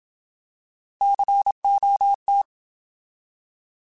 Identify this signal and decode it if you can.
Morse code, heard as a keyed tone, about 27 wpm, 800 Hz. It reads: COT
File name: Alphabetical